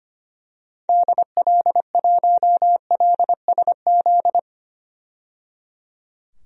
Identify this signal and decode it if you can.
Morse code, heard as a keyed tone, about 25 wpm, 700 Hz. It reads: DL1LSZ
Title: DXCC-100 Call Signs - 6 Characters - 25wpm